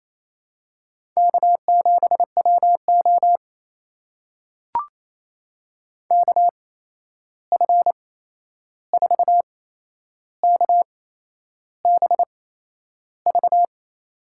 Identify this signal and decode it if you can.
Morse code, heard as a keyed tone, about 28 wpm, spaced as for 11 wpm, 700 Hz. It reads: K7WO KF4KBV
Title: USA Call Signs - 8x Character Spacing - 28wpm